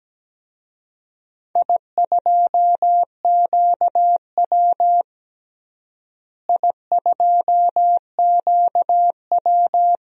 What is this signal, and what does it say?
Morse code, heard as a keyed tone, about 17 wpm, 700 Hz. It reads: I2QW I2QW
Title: DXCC-100 Call Signs - 4 Characters - Repeated 3x - 17wpm